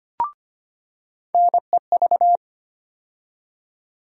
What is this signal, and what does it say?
Morse code, heard as a keyed tone, about 25 wpm, 700 Hz. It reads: NEV
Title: USA Call Sign Postfixes - 3 Letters 25wpm